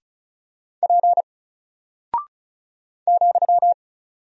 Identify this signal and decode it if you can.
Morse code, heard as a keyed tone, about 36 wpm, 700 Hz. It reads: P ,
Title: Single Character (Letter:Number:Punctuation:Prosign) 35wpm